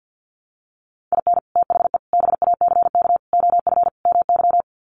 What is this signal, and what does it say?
Morse code, heard as a keyed tone, about 50 wpm, 700 Hz. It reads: IN THE BACK OF MY
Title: Sets of 5 Words Encore 50wpm